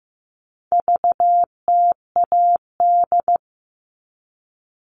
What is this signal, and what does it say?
Morse code, heard as a keyed tone, about 15 wpm, 700 Hz. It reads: VTAD
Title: All Cut Numbers - 4-Digit 15wpm